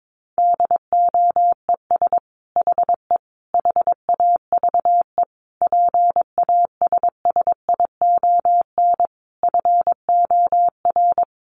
Morse code, heard as a keyed tone, about 22 wpm, 700 Hz. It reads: DOES HE HAVE PASSION FOR
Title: Sentences from Top 1300 Words 22wpm